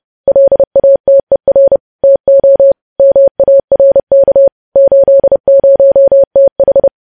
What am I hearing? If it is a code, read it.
Morse code, heard as a keyed tone, about 30 wpm, 550 Hz. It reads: LATER TO MARK 80TH
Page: News Headlines in Morse Code at 30 WPM